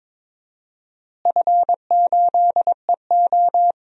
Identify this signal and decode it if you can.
Morse code, heard as a keyed tone, about 21 wpm, 700 Hz. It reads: F8EO
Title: DXCC-100 Call Signs - 4 Characters - Repeated 3x - 22wpm